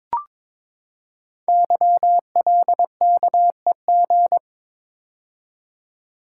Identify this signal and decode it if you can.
Morse code, heard as a keyed tone, about 22 wpm, 700 Hz. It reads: YLKEG
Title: Code Groups - 5 Letters - 22wpm